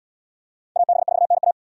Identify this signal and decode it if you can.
Morse code, heard as a keyed tone, about 68 wpm, 700 Hz. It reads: AB4NA